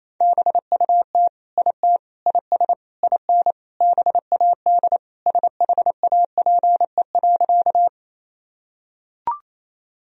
Morse code, heard as a keyed tone, about 28 wpm, 700 Hz. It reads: BUT IT IS IN BAD SHAPE.
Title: Sentences from Top 1000 Words - Review Part 3 - 28wpm